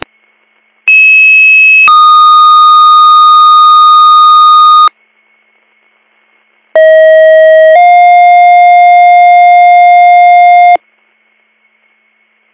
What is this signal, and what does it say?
Two-tone page, heard as a one-second tone, then a three-second tone, about 2600 then 1230 Hz, about 640 then 710 Hz. It is heard from 0.9 to 10.8 s.